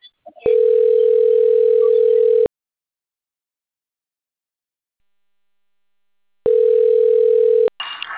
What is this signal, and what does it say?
Ringback tone (North America), 440 + 480 Hz.